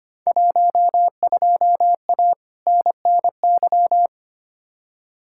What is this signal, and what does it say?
Morse code, heard as a keyed tone, about 25 wpm, 700 Hz. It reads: 12A NNY
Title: Field Day Exchanges Encore - 25wpm